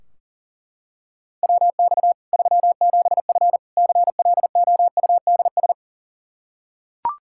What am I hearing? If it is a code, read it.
Morse code, heard as a keyed tone, about 37 wpm, 700 Hz. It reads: WX 37F CLOUDS
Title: Practice QSO - Exchange 2 - Info - 40wpm